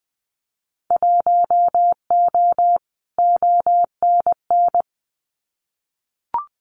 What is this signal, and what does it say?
Morse code, heard as a keyed tone, about 20 wpm, 700 Hz. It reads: 1O ONN
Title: Winter Field Day Exchanges Encore 20wpm